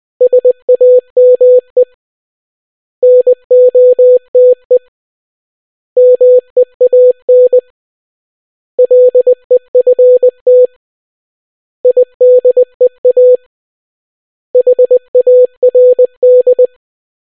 Morse code, heard as a keyed tone, about 20 wpm, 500 Hz. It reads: SAME NOTE MEAN LEFT IDEA HARD